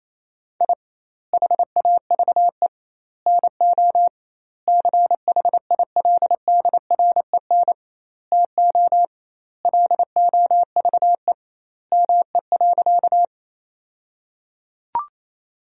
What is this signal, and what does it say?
Morse code, heard as a keyed tone, about 28 wpm, 700 Hz. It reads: I HAVE NO CHILDREN TO LOVE ME.
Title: 2x Word-Spacing - Sentences from Top 400 Words - 28wpm